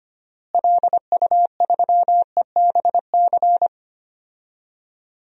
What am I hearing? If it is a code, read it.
Morse code, heard as a keyed tone, about 25 wpm, 700 Hz. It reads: LU3EBC
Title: DXCC-100 Call Signs - 6 Characters - Repeated 3x - 25wpm